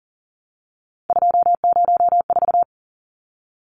Morse code, heard as a keyed tone, about 40 wpm, 700 Hz. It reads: 204
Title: Numbers - 3-Digit 40wpm